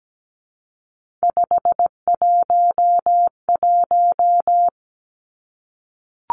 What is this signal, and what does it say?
Morse code, heard as a keyed tone, about 17 wpm, 700 Hz. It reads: 511
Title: Three Challenging Letter-Number Combinations 17wpm